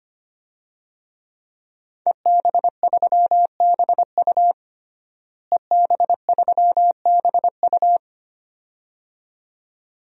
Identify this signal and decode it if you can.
Morse code, heard as a keyed tone, about 25 wpm, 700 Hz. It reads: EB3BU EB3BU
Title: DXCC-100 Call Signs - 5 Characters - Repeated 2x - 25wpm